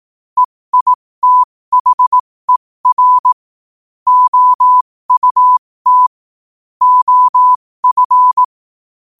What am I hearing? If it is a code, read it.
Morse code, heard as a keyed tone, about 18 wpm, spaced as for 15 wpm, 1000 Hz. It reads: EITHER OUT OF
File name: qotdfiles / 15wpm